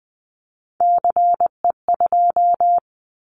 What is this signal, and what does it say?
Morse code, heard as a keyed tone, about 20 wpm, 700 Hz. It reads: CE2
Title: Callsign Prefixes for Top 100 DXCC - 20wpm